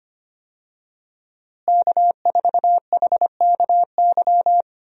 Morse code, heard as a keyed tone, about 25 wpm, 700 Hz. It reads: K4HKY
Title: USA Call Signs - 5 Characters - Repeated 2x - 25wpm